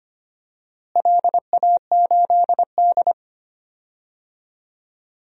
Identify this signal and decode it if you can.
Morse code, heard as a keyed tone, about 25 wpm, 700 Hz. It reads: LA8D
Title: DXCC-100 Call Signs - 4 Characters - Repeated 2x - 25wpm